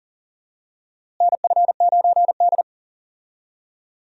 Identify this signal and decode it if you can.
Morse code, heard as a keyed tone, about 40 wpm, 700 Hz. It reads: NF9D
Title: USA Call Signs - 4 Characters - Repeated 2x - 40wpm